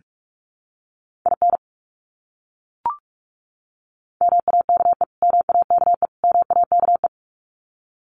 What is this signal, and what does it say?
Morse code, heard as a keyed tone, about 44 wpm, 700 Hz. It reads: IN MAKE MAKE MAKE
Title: Top 300 Words Repeated 45wpm